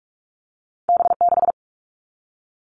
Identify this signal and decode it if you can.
Morse code, heard as a keyed tone, about 46 wpm, 700 Hz. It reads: B6
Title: All Cut Numbers - 2-Digit 45wpm